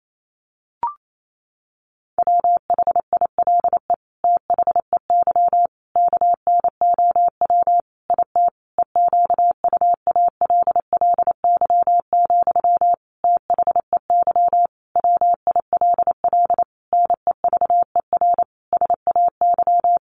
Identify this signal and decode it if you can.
Morse code, heard as a keyed tone, about 28 wpm, 700 Hz. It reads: WHILE THEY KNOW IT EQUALLY, THEY WILL NEVER SAY
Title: Sentences from Top 1400 Words 28wpm